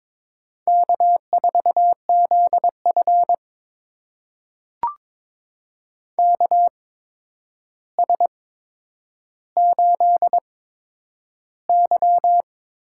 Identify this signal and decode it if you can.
Morse code, heard as a keyed tone, about 22 wpm, spaced as for 9 wpm, 700 Hz. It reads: K4ZF KS8Y
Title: USA Call Signs - 4 Characters - 8x Character Spacing - 22wpm